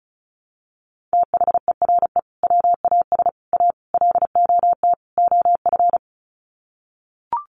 Morse code, heard as a keyed tone, about 34 wpm, 700 Hz. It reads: THERE WAS A LOT OF
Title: Sets of 5 Words 35wpm